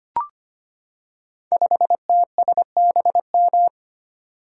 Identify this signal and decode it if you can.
Morse code, heard as a keyed tone, about 25 wpm, 700 Hz. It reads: 5TSBM
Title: Code Groups - 5 Letter:Numbers - 25wpm